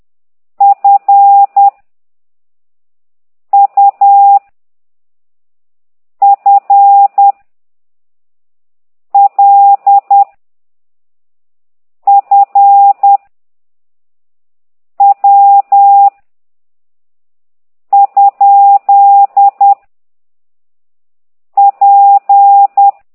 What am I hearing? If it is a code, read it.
Morse code, heard as a keyed tone, about 10 wpm, 800 Hz. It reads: F U F L F W ? P
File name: ct2idw / exerc8